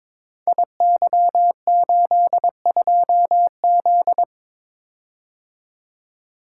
Morse code, heard as a keyed tone, about 22 wpm, 700 Hz. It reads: IY82Z
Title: Code Groups - 5 Letter:Numbers - 22wpm